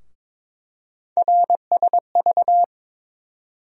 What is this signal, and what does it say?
Morse code, heard as a keyed tone, about 22 wpm, 700 Hz. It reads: RSV
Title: USA Call Sign Postfixes - 3 Letters 22wpm